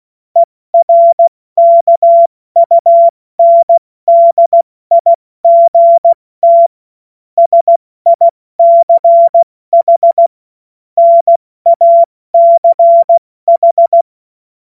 Morse code, heard as a keyed tone, about 16 wpm, 680 Hz. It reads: ERKUNDIGT SICH NACH
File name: Morsekurs / Witze / Witz 19. April 2022 70 F80